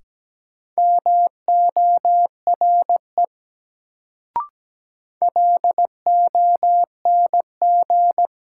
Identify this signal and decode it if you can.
Morse code, heard as a keyed tone, about 17 wpm, 700 Hz. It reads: MORE LONG
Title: Sets of 2 Words - New Word Drill - 17wpm